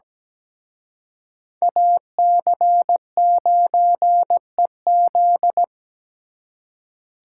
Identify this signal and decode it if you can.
Morse code, heard as a keyed tone, about 20 wpm, 700 Hz. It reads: AC9EZ
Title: Speed Racing - USA Call Signs 17wpm